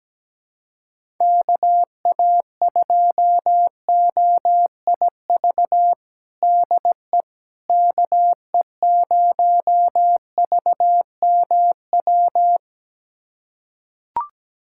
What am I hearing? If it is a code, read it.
Morse code, heard as a keyed tone, about 17 wpm, 700 Hz. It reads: KA2OIV DE KE0VMW
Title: USA Callsign DE Callsign - 6 Characters - 17wpm